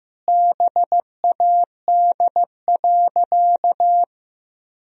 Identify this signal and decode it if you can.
Morse code, heard as a keyed tone, about 15 wpm, 700 Hz. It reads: BAD.
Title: Sentences from Top 1000 Words - Review Part 3 - 15wpm